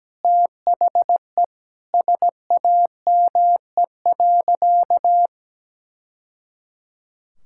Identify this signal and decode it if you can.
Morse code, heard as a keyed tone, about 17 wpm, 700 Hz. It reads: THE SAME.
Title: Sentences from Top 1000 Words - Review Part 5 - 17wpm